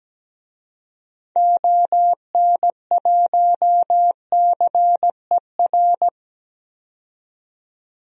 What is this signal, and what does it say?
Morse code, heard as a keyed tone, about 17 wpm, 700 Hz. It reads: ON1CER
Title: DXCC-100 Call Signs - 6 Characters - 17wpm